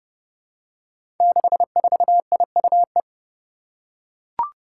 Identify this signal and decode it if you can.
Morse code, heard as a keyed tone, about 30 wpm, 700 Hz. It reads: 64IUE
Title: Five Challenging Letter-Number Combinations 30wpm